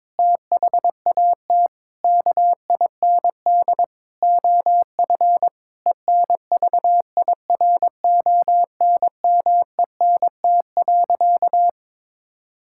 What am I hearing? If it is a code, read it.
Morse code, heard as a keyed tone, about 22 wpm, 700 Hz. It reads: THAT KIND OF ENVIRONMENT.
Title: Sentences from Top 1000 Words - Review Part 5 - 22wpm